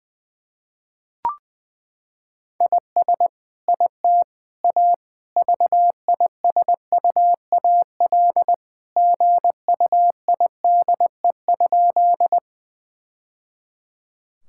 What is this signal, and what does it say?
Morse code, heard as a keyed tone, about 20 wpm, 700 Hz. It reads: IS IT A VISUAL GUIDE?
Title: Sentences from Top 1000 Words - Review Part 6 - 20wpm